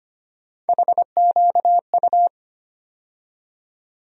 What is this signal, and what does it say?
Morse code, heard as a keyed tone, about 25 wpm, 700 Hz. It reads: HQU
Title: USA Call Sign Postfixes - 3 Letters 25wpm